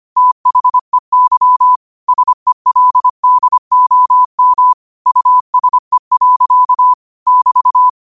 Morse code, heard as a keyed tone, about 25 wpm, 1000 Hz. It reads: THEY SELDOM USE. <BT>